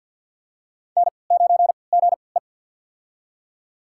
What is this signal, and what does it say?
Morse code, heard as a keyed tone, about 46 wpm, spaced as for 27 wpm, 700 Hz. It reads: N9GE
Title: USA Call Signs - 4 Characters - 3x Character Spacing - 50wpm